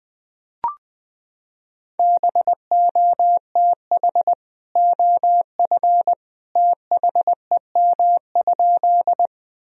Morse code, heard as a keyed tone, about 20 wpm, 700 Hz. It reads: BOTH OF THEM?
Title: Sentences from Top 1000 Words - Review Part 1 - 20wpm